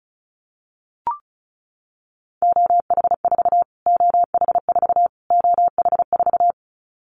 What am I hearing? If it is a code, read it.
Morse code, heard as a keyed tone, about 34 wpm, 700 Hz. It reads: OH4 OH4 OH4
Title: Callsign Prefixes for Top 25 DXCC - Repeated 3x - 35wpm